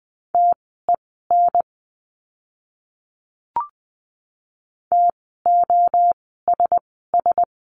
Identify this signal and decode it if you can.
Morse code, heard as a keyed tone, about 20 wpm, spaced as for 14 wpm, 700 Hz. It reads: TEN TOSS
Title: 054 Lesson 10f - Letter r - Words 20wpm